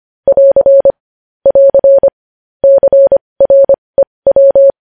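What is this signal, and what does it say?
Morse code, heard as a keyed tone, about 25 wpm, spaced as for 20 wpm, 550 Hz. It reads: <AR> <AR> CREW
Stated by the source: Morse practice sending news headlines at 20 wpm.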